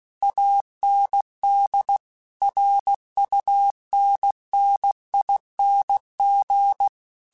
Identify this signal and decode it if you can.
Morse code, heard as a keyed tone, about 16 wpm, 800 Hz. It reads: AND RUNNING
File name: QSOs / QSOs-mp3s / WD0FBT 1@15